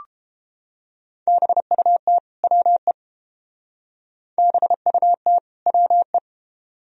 Morse code, heard as a keyed tone, about 29 wpm, 700 Hz. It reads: BUT WE BUT WE